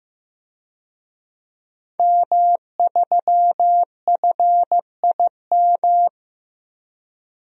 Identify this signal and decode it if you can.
Morse code, heard as a keyed tone, about 15 wpm, 700 Hz. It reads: M3FIM
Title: DXCC-100 Call Signs - 5 Characters - 15wpm